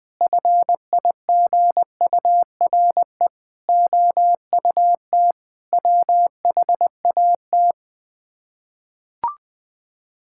Morse code, heard as a keyed tone, about 20 wpm, 700 Hz. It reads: FIGURE OUT WHAT
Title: Build A Phrase - 5-Word Phrases - 20wpm